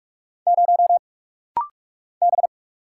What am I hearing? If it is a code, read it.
Morse code, heard as a keyed tone, about 46 wpm, 700 Hz. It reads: 0 B
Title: Single Letter-Number Alt - Rapid-Fire 45wpm